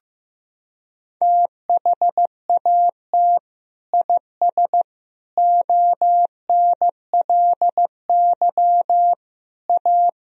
Morse code, heard as a keyed tone, about 15 wpm, 700 Hz. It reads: THAT IS ONLY A
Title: Sentences from Top 1000 Words - Review Part 4 - 15wpm